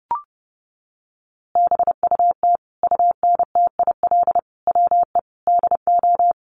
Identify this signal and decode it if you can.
Morse code, heard as a keyed tone, about 30 wpm, 700 Hz. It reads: BUT UNTIL WE DO
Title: A Sound Of Thunder 30wpm